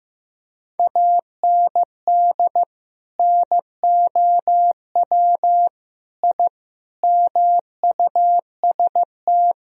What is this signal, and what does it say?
Morse code, heard as a keyed tone, about 15 wpm, 700 Hz. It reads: AND NOW I MUST
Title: Sentences from Top 1000 Words - Review Part 3 - 15wpm